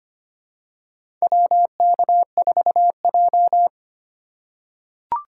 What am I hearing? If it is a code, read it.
Morse code, heard as a keyed tone, about 25 wpm, 700 Hz. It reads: WK4J
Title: USA Call Signs - 4 Characters - 4x Character Spacing - 25wpm